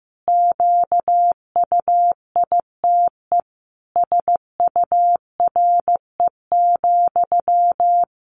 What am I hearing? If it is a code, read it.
Morse code, heard as a keyed tone, about 15 wpm, 700 Hz. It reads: QUITE SURE,
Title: Sentences from Top 700 Words 15wpm